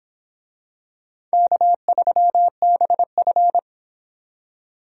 Morse code, heard as a keyed tone, about 26 wpm, 700 Hz. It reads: K3BF